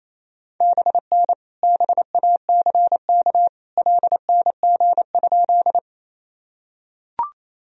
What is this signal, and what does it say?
Morse code, heard as a keyed tone, about 28 wpm, 700 Hz. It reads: BN BACK LNG?
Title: Sentences from Top 200 Words and Abbreviations 28wpm